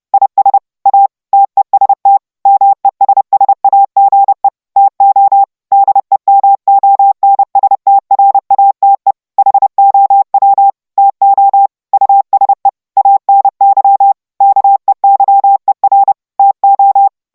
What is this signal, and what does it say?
Morse code, heard as a keyed tone, about 30 wpm, 790 Hz. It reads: IS A TEST MESSAGE TO DEMONSTRATE HOW TO USE ANY KEYER TO